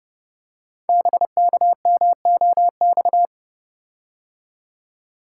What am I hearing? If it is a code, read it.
Morse code, heard as a keyed tone, about 30 wpm, 700 Hz. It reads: BKMOX
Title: Code Groups - 5 Letters - 30wpm